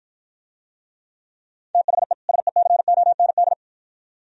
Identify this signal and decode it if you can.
Morse code, heard as a keyed tone, about 53 wpm, 700 Hz. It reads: THE SECOND